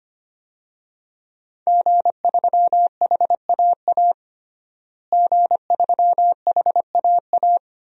Morse code, heard as a keyed tone, about 25 wpm, 700 Hz. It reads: G3HAA G3HAA
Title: DXCC-100 Call Signs - 5 Characters - Repeated 2x - 25wpm